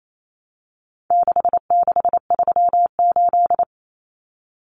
Morse code, heard as a keyed tone, about 28 wpm, 700 Hz. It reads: 6638
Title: Numbers - 4-Digit 28wpm